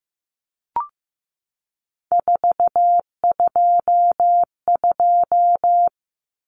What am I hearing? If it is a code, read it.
Morse code, heard as a keyed tone, about 15 wpm, 700 Hz. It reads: 422
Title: Numbers - 3-Digit 15wpm